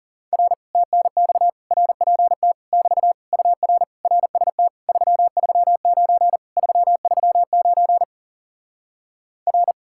Morse code, heard as a keyed tone, about 40 wpm, 700 Hz. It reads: R TNX RPT <BT> UR RST 339 339 R